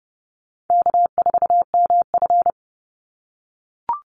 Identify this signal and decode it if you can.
Morse code, heard as a keyed tone, about 30 wpm, 700 Hz. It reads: K4MF
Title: NA SOTA Chaser Callsigns 30wpm